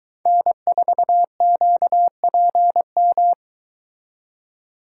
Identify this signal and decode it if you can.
Morse code, heard as a keyed tone, about 22 wpm, 700 Hz. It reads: N4QPM